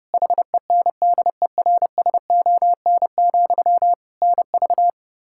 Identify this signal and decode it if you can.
Morse code, heard as a keyed tone, about 31 wpm, 700 Hz. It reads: HENDERSON, NV
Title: Practice QSO - Exchange 1 - Beginning - 30wpm